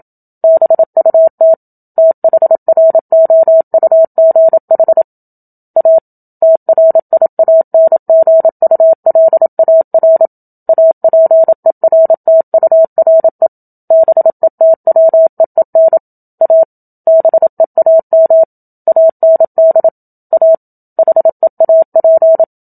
Morse code, heard as a keyed tone, about 27 wpm, 650 Hz. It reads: BUT THROUGH A TRIANGULAR APERTURE BETWEEN A BEAM AND A HEAP
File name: Book Two / 01-Chapter 1 - 27 WPM